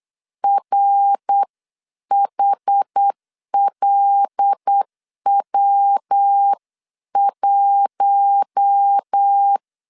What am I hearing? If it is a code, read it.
Morse code, heard as a keyed tone, about 8 wpm, 800 Hz. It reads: RHLW1
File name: CW / DET / 1 6 1500C S B 8wpm 5142